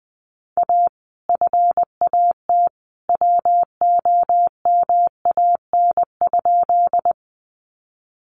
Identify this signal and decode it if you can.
Morse code, heard as a keyed tone, about 20 wpm, 700 Hz. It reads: A FAT WOMAN?
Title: Sentences from Top 1100 Words 20wpm